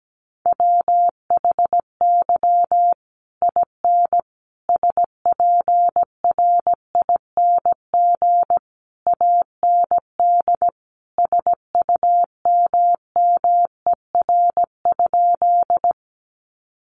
Morse code, heard as a keyed tone, about 17 wpm, 700 Hz. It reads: WHY IN SPRING AND SUMMER?
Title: Sentences from Top 1200 Words 17wpm